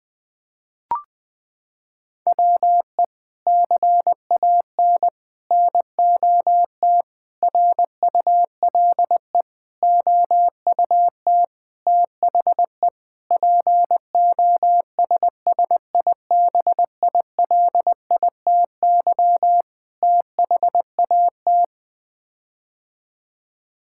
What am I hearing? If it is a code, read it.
Morse code, heard as a keyed tone, about 20 wpm, 700 Hz. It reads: WE CAN NOT RULE OUT THE POSSIBILITY THAT
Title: Sets of 8 Words - 20wpm